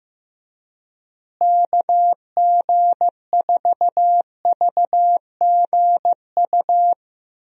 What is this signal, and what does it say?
Morse code, heard as a keyed tone, about 15 wpm, 700 Hz. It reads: KG4VGU
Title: USA Call Signs - 6 Characters - Repeated 2x - 15wpm